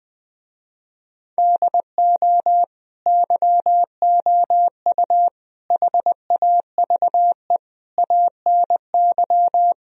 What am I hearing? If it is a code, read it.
Morse code, heard as a keyed tone, about 20 wpm, 700 Hz. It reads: DO YOU HAVE ANY